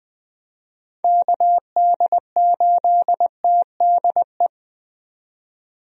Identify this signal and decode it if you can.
Morse code, heard as a keyed tone, about 20 wpm, 700 Hz. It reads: KD8TDE
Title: USA Call Signs - 6 Characters - 20wpm